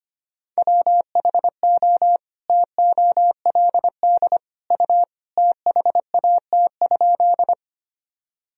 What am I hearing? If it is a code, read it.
Morse code, heard as a keyed tone, about 25 wpm, 700 Hz. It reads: WHO TOLD U THAT?
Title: Sentences from Top 400 Words and Abbreviations 25wpm